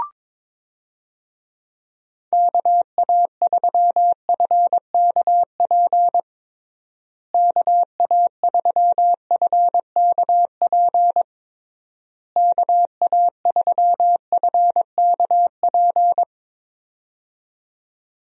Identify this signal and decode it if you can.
Morse code, heard as a keyed tone, about 22 wpm, 700 Hz. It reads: KA3FKP KA3FKP KA3FKP
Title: USA Call Signs - 6 Characters - Repeated 3x - 22wpm